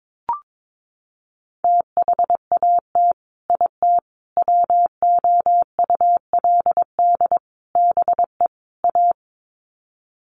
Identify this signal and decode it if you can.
Morse code, heard as a keyed tone, about 22 wpm, 700 Hz. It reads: THAT IT WOULD BE A
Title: Sets of 5 Words Encore2 22wpm